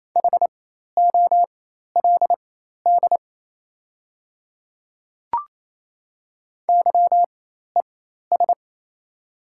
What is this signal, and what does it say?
Morse code, heard as a keyed tone, about 28 wpm, spaced as for 13 wpm, 700 Hz. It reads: HOLD YES
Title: 4x Character Spacing - Top 500 Words - 28wpm